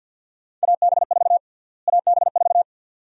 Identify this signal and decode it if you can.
Morse code, heard as a keyed tone, about 46 wpm, 700 Hz. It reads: AB4 AB4
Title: USA Call Sign Prefixes - Repeated 2x - 50wpm